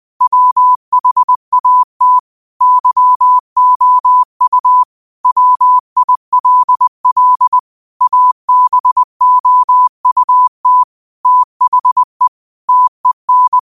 Morse code, heard as a keyed tone, about 20 wpm, 1000 Hz. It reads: WHAT YOU WILL ABOUT THE TEN